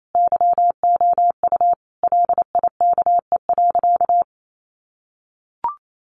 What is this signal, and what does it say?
Morse code, heard as a keyed tone, about 28 wpm, 700 Hz. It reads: YOU LIKE.
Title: Sentences from Top 1400 Words 28wpm